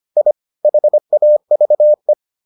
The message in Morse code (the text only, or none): I HAVE